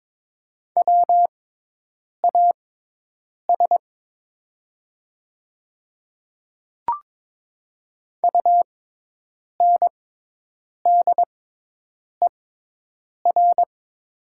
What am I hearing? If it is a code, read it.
Morse code, heard as a keyed tone, about 22 wpm, spaced as for 8 wpm, 700 Hz. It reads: WAS UNDER
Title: 6x Character Spacing - Top 200 Words - 22wpm